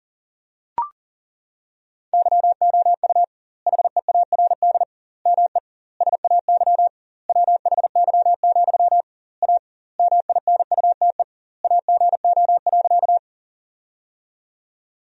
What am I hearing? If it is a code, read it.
Morse code, heard as a keyed tone, about 40 wpm, 700 Hz. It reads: YOU HEARD ME SAY WHY, A MINUTE AGO.
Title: 2x Word-Spacing - Sentences from Top 500 Words - 40wpm